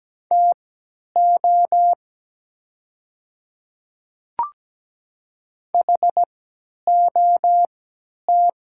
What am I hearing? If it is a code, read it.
Morse code, heard as a keyed tone, about 17 wpm, spaced as for 10 wpm, 700 Hz. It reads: TO HOT
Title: 3x Character Spacing - Top 100 Words - 17wpm